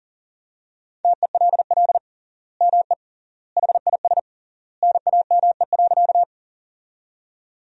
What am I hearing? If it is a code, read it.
Morse code, heard as a keyed tone, about 40 wpm, 700 Hz. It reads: TELL ME HIS NAME.
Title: 3x Word-Spacing - Sentences from Top 200 Words - 40wpm